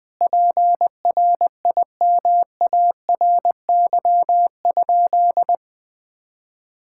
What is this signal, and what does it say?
Morse code, heard as a keyed tone, about 20 wpm, 700 Hz. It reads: PRIMARY?
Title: Sentences from Top 1000 Words - Review Part 5 - 20wpm